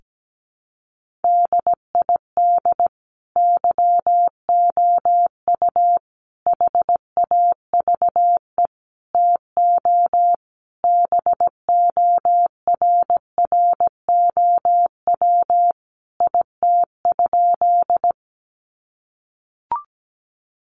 Morse code, heard as a keyed tone, about 17 wpm, 700 Hz. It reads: DID YOU HAVE TO BORROW IT?
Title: Sentences from Top 1500 Words 17wpm